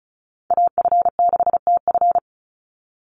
Morse code, heard as a keyed tone, about 34 wpm, 700 Hz. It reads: AF6TF
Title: USA Call Signs 35wpm